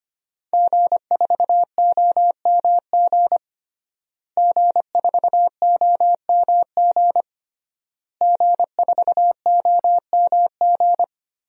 Morse code, heard as a keyed tone, about 25 wpm, 700 Hz. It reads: G4OMG G4OMG G4OMG
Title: DXCC-100 Call Signs - 5 Characters - Repeated 3x - 25wpm